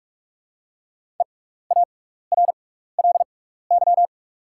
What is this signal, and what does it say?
Morse code, heard as a keyed tone, about 44 wpm, spaced as for 15 wpm, 700 Hz. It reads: EARLY
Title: 6x Character Spacing - Top 500 Words - 45wpm